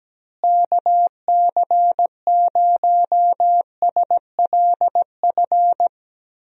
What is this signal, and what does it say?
Morse code, heard as a keyed tone, about 17 wpm, 700 Hz. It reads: KC0SLF
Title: USA Call Signs - 6 Characters - Repeated 3x - 17wpm